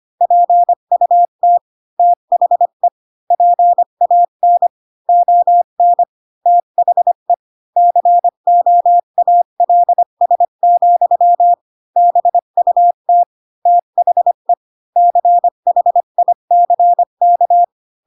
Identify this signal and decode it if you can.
Morse code, heard as a keyed tone, about 25 wpm, 700 Hz. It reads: PUT THE PAN ON THE COALS, BUT THE CHICK
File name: Pinocchio / Pinocchio ch7 20 25 ik1hge